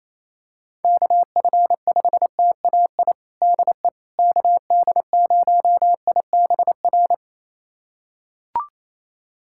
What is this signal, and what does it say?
Morse code, heard as a keyed tone, about 28 wpm, 700 Hz. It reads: KF5TAI DE KD0IBR
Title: USA Callsign DE Callsign 28wpm